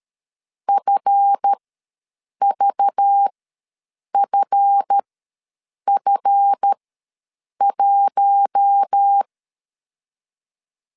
Morse code, heard as a keyed tone, about 13 wpm, spaced as for 7 wpm, 800 Hz. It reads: FVFF1
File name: LEZIONI / 7 GF9Z / 7 GF ws 12wpm 3140